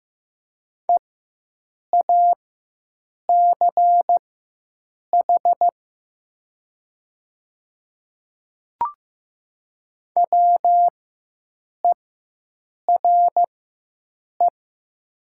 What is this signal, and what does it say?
Morse code, heard as a keyed tone, about 15 wpm, spaced as for 7 wpm, 700 Hz. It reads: EACH WERE
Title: 4x Character Spacing - Top 100 Words - 15wpm